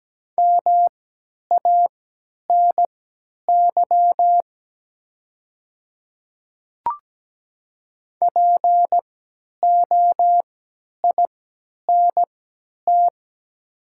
Morse code, heard as a keyed tone, about 17 wpm, spaced as for 10 wpm, 700 Hz. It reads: MANY POINT
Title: 3x Character Spacing - Top 300 Words - 17wpm